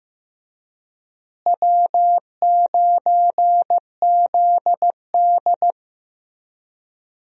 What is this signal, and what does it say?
Morse code, heard as a keyed tone, about 15 wpm, 700 Hz. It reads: W9ZD
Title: USA Call Signs - 4 Characters - Repeated 2x - 15wpm